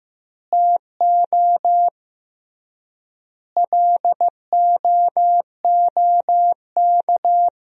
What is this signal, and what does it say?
Morse code, heard as a keyed tone, about 15 wpm, 700 Hz. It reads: TO LOOK
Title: 3x Word-Spacing - Sentences from Top 200 Words - 15wpm